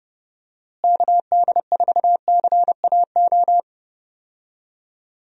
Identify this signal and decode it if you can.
Morse code, heard as a keyed tone, about 30 wpm, 700 Hz. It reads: KD4CAO
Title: USA Call Signs - 6 Characters - Repeated 2x - 30wpm